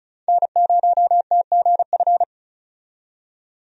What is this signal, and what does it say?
Morse code, heard as a keyed tone, about 36 wpm, 700 Hz. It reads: N0TGF
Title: USA Call Signs - 5 Characters - 35wpm